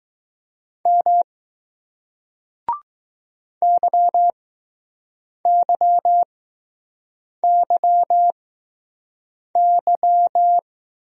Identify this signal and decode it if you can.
Morse code, heard as a keyed tone, about 20 wpm, spaced as for 6 wpm, 700 Hz. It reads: M YYYY